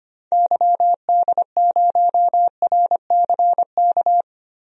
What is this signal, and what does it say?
Morse code, heard as a keyed tone, about 25 wpm, 700 Hz. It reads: YD0RCK
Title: DXCC-100 Call Signs - 6 Characters - Repeated 3x - 25wpm